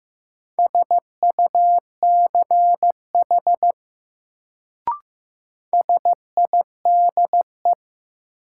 Morse code, heard as a keyed tone, about 15 wpm, 700 Hz. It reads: SUCH SIDE
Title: Sets of 2 Words - New Word Drill - 15wpm